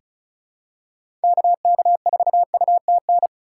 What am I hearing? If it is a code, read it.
Morse code, heard as a keyed tone, about 36 wpm, 700 Hz. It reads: KK4UTN
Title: USA Call Signs - 6 Characters - 3x Character Spacing - 35wpm